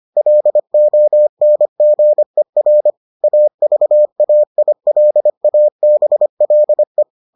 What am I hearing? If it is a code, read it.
Morse code, heard as a keyed tone, about 25 wpm, 600 Hz. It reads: LONGER AVAILABLE